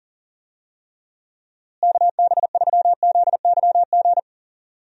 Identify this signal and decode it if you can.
Morse code, heard as a keyed tone, about 40 wpm, 700 Hz. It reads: KB3ZYG
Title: USA Call Signs - 6 Characters - Repeated 3x - 40wpm